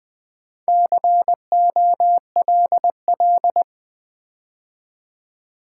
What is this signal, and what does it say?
Morse code, heard as a keyed tone, about 20 wpm, 700 Hz. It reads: COLL
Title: Build A Word - 2 Letters At A Time - 8-Letter Words - 20wpm